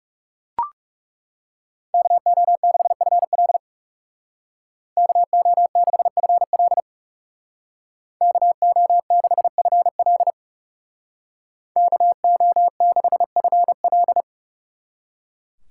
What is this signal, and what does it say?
Morse code, heard as a keyed tone, about 36 wpm, 700 Hz. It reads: KO6FL KO6FL KO6FL KO6FL
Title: Speed Racing - USA 2x2 Call Signs 30wpm